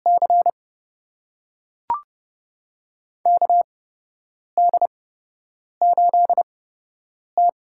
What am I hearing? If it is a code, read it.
Morse code, heard as a keyed tone, about 30 wpm, spaced as for 8 wpm, 700 Hz. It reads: C KD8T
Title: USA Call Signs - 8x Character Spacing - 30wpm